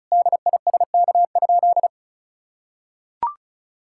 Morse code, heard as a keyed tone, about 36 wpm, 700 Hz. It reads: DISK?
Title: Sentences from Top 1000 Words - Review Part 6 - 35wpm